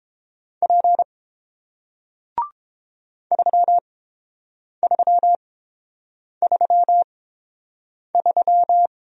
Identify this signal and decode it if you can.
Morse code, heard as a keyed tone, about 31 wpm, spaced as for 7 wpm, 700 Hz. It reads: P 3333